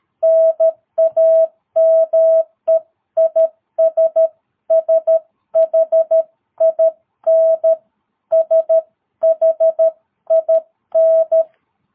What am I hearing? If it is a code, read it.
Morse code, heard as a keyed tone, about 13 wpm, 660 Hz. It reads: NAME IS SHIN SHIN
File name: jh8cbh / 2022cw